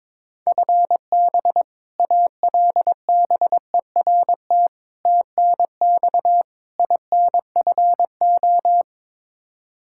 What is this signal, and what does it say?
Morse code, heard as a keyed tone, about 22 wpm, 700 Hz. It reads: FB ALBERT TNX INFO
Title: Practice QSO - Exchange 3 - More Info - 22wpm